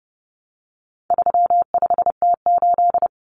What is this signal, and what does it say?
Morse code, heard as a keyed tone, about 30 wpm, 700 Hz. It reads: 35T8
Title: Common Cut Numbers - 4-Digit 30wpm